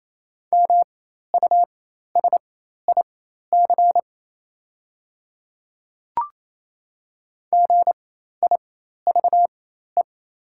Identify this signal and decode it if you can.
Morse code, heard as a keyed tone, about 28 wpm, spaced as for 13 wpm, 700 Hz. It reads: MUSIC GIVE
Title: 4x Character Spacing - Top 300 Words - 28wpm